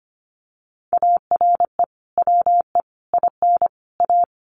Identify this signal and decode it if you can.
Morse code, heard as a keyed tone, about 25 wpm, 700 Hz. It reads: ARE WE IN A
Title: Sentences from Top 1300 Words 25wpm